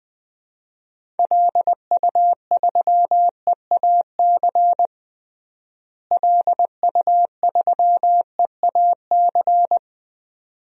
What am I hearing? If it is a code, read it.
Morse code, heard as a keyed tone, about 20 wpm, 700 Hz. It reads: LU3EAC LU3EAC
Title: DXCC-100 Call Signs - 6 Characters - Repeated 2x - 20wpm